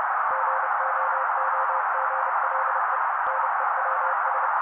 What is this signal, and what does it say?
Morse code, heard as a keyed tone, about 30 wpm, 530 Hz. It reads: GOOGLE NEWS
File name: daily / 29wpm noise